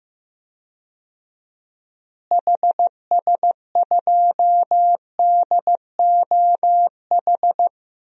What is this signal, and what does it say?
Morse code, heard as a keyed tone, about 15 wpm, 700 Hz. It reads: HS2DOH
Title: DXCC-100 Call Signs - 6 Characters - Repeated 2x - 15wpm